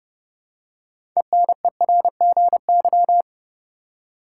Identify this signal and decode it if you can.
Morse code, heard as a keyed tone, about 30 wpm, 700 Hz. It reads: ENERGY
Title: Six Letter Words - 30wpm